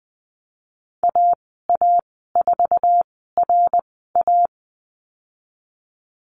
Morse code, heard as a keyed tone, about 20 wpm, spaced as for 15 wpm, 700 Hz. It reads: AA4RA
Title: 076 Lesson 13g - Letter L - Call Signs 20wpm